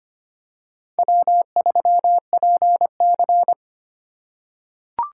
Morse code, heard as a keyed tone, about 25 wpm, 700 Hz. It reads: W3PC
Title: USA Call Signs - 2x Character Spacing - 25wpm